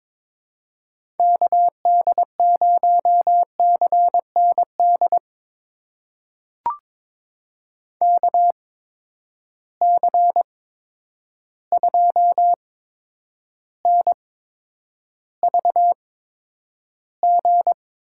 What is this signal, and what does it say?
Morse code, heard as a keyed tone, about 21 wpm, spaced as for 9 wpm, 700 Hz. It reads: KD0CND KC2NVG
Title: USA Call Signs - 6 Characters - 8x Character Spacing - 22wpm